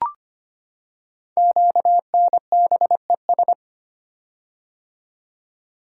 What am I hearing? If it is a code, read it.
Morse code, heard as a keyed tone, about 25 wpm, 700 Hz. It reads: QNBES
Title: Code Groups - 5 Letters - 25wpm